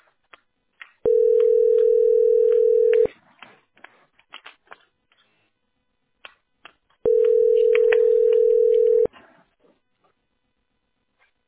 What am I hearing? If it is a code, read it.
Ringback tone (North America), 440 + 480 Hz.